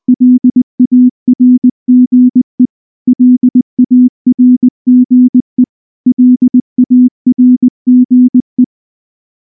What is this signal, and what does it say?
Morse code, heard as a keyed tone, about 19 wpm, 250 Hz. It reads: LARGE LARGE LARGE
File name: morse / morse-training-20WPM-1